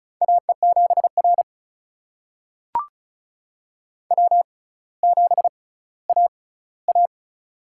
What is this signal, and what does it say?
Morse code, heard as a keyed tone, about 36 wpm, spaced as for 18 wpm, 700 Hz. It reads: AE7R W7AA
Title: USA Call Signs - 6x Character Spacing - 35wpm